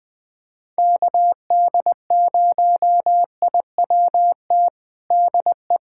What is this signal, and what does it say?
Morse code, heard as a keyed tone, about 20 wpm, 700 Hz. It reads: KD0IWT DE
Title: USA Callsign DE Callsign - 6 Characters - 20wpm